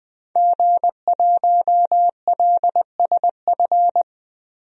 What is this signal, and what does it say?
Morse code, heard as a keyed tone, about 20 wpm, 700 Hz. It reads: G1LSF
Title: DXCC-100 Call Signs - 5 Characters - 20wpm